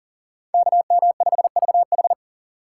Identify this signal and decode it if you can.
Morse code, heard as a keyed tone, about 40 wpm, 700 Hz. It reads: KM5VH
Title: USA Call Signs - 5 Characters - 6x Character Spacing - 40wpm